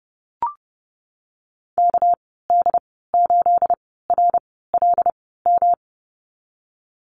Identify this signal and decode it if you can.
Morse code, heard as a keyed tone, about 32 wpm, spaced as for 17 wpm, 700 Hz. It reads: KD8RLM
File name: Lesson 37g - Letter x - Call Signs 30wpm